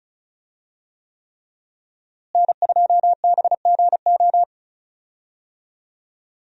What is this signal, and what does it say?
Morse code, heard as a keyed tone, about 34 wpm, 700 Hz. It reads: N2BGO
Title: USA Call Signs - 5 Characters - Repeated 3x - 35wpm